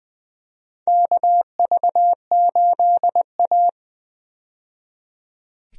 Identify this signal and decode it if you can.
Morse code, heard as a keyed tone, about 20 wpm, 700 Hz. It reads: KV8A
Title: USA Call Signs - 4 Characters - 20wpm